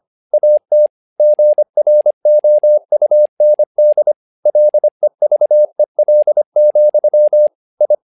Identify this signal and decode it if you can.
Morse code, heard as a keyed tone, about 25 wpm, 600 Hz. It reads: AT GROUND LEVEL, I
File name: learning center / tower trouble / tower trouble 25wpm